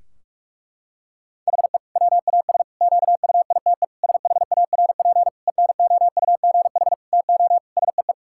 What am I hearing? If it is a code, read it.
Morse code, heard as a keyed tone, about 46 wpm, 700 Hz. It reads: HE WAS QUITE SHARP ENOUGH TO SEE
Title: Sentences from Top 1600 Words 45wpm